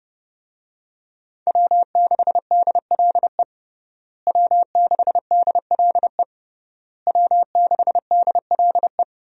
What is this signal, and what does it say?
Morse code, heard as a keyed tone, about 30 wpm, 700 Hz. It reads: W6DLE W6DLE W6DLE
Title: USA Call Signs - 5 Characters - Repeated 3x - 30wpm